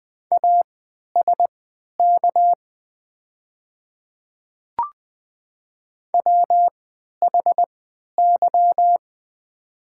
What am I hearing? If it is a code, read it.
Morse code, heard as a keyed tone, about 20 wpm, spaced as for 11 wpm, 700 Hz. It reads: ASK WHY
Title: 3x Character Spacing - Top 200 Words - 20wpm